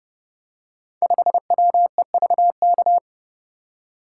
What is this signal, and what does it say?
Morse code, heard as a keyed tone, about 31 wpm, 700 Hz. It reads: 5WEVK
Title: Code Groups - 5 Letter:Numbers - 30wpm